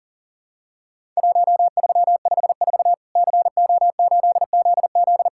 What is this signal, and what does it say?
Morse code, heard as a keyed tone, about 40 wpm, 700 Hz. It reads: 1354 CO8ZZ
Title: Worked All Europe DX Contest - QTC Exchange - 40wpm